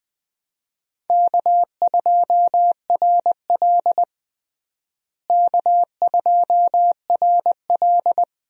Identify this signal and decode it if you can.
Morse code, heard as a keyed tone, about 20 wpm, 700 Hz. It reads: K2RL K2RL
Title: USA Call Signs - 4 Characters - Repeated 2x - 20wpm